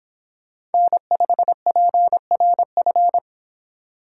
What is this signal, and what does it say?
Morse code, heard as a keyed tone, about 30 wpm, 700 Hz. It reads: N5PRF